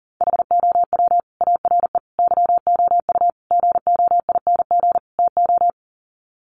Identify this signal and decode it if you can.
Morse code, heard as a keyed tone, about 40 wpm, 700 Hz. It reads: HOW ARE YOU GOING TO
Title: Sets of 5 Words 40wpm